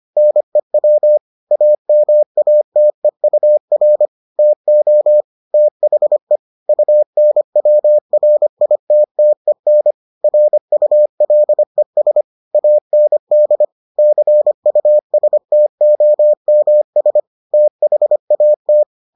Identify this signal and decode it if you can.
Morse code, heard as a keyed tone, about 25 wpm, 600 Hz. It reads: NEW AMATEUR TO THE UNWRITTEN RULES AND CUSTOMS THAT